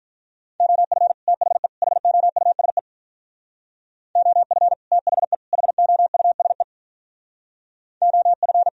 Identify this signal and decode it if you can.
Morse code, heard as a keyed tone, about 46 wpm, 700 Hz. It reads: OF THE HOUSE OF THE HOUSE OF